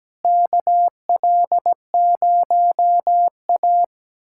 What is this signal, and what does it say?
Morse code, heard as a keyed tone, about 17 wpm, 700 Hz. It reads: KL0A
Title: USA Callsign DE Callsign - 4 Characters - 17wpm